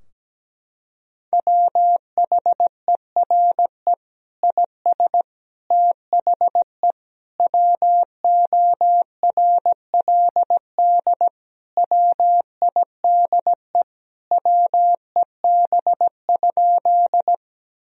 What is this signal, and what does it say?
Morse code, heard as a keyed tone, about 17 wpm, 700 Hz. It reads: WHERE IS THE WORLD WIDE WEB?
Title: Sentences from Top 1000 Words - Review Part 4 - 17wpm